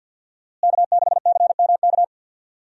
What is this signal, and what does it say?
Morse code, heard as a keyed tone, about 53 wpm, 700 Hz. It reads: K6CMK
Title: USA Call Signs - 5 Characters - Repeated 3x - 50wpm